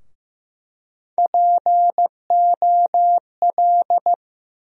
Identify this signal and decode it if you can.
Morse code, heard as a keyed tone, about 15 wpm, 700 Hz. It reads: POL
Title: USA Call Sign Postfixes - 3 Letters 15wpm